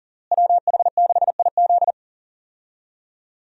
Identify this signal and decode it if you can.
Morse code, heard as a keyed tone, about 40 wpm, 700 Hz. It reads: WH6IZ
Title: USA Call Signs - 5 Characters - Repeated 2x - 40wpm